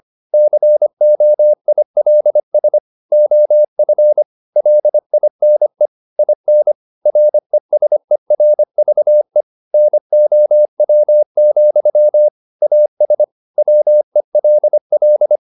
Morse code, heard as a keyed tone, about 25 wpm, 600 Hz. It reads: COILS OF LINE IN RESERVE NOW, AS WELL
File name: the old man and the sea part 1 25wpm